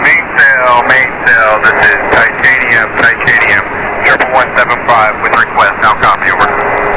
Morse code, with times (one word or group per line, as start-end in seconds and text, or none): none